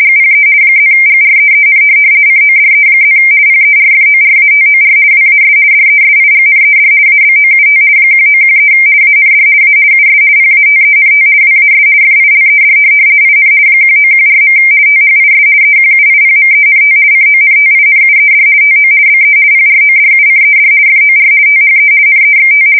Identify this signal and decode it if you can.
RTTY, about 50 baud. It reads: GPE +GREEN CODE+ PERFORATOR WAS DESIGNED ABOUT 1913 AND A US PATENT WAS FILED IN 1914. THIS EQUIPMENT CONTINUED TO BE PRODUCED FOR THE NEXT
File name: fsk170-50B / teletype-corp